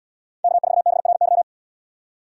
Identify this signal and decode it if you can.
Morse code, heard as a keyed tone, about 77 wpm, 700 Hz. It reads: K4BRY